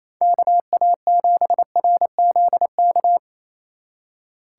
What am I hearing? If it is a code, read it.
Morse code, heard as a keyed tone, about 34 wpm, 700 Hz. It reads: KA7RZK
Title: Speed Racing - USA 2x3 Call Signs 28wpm